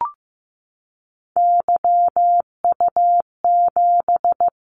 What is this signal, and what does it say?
Morse code, heard as a keyed tone, about 15 wpm, 700 Hz. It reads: YU7
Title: Callsign Prefixes for Top 25 DXCC - Repeated 3x - 15wpm